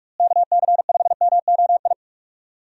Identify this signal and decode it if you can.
Morse code, heard as a keyed tone, about 46 wpm, 700 Hz. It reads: KC5MOI
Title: USA Call Signs - 6 Characters - Repeated 3x - 45wpm